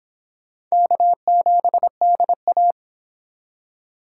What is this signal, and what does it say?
Morse code, heard as a keyed tone, about 25 wpm, 700 Hz. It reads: K7DA